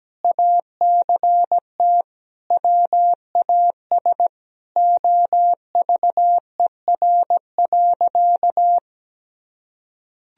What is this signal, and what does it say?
Morse code, heard as a keyed tone, about 17 wpm, 700 Hz. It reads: ACT WAS OVER.
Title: Sentences from Top 1000 Words - Review Part 1 - 17wpm